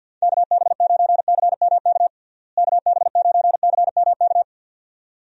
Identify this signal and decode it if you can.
Morse code, heard as a keyed tone, about 51 wpm, 700 Hz. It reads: KB9CMK KB9CMK
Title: USA Call Signs - 6 Characters - Repeated 2x - 50wpm